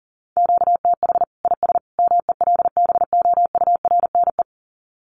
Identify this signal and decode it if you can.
Morse code, heard as a keyed tone, about 40 wpm, 700 Hz. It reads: QTH IS MELBOURNE
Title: Common QSO Phrases 40wpm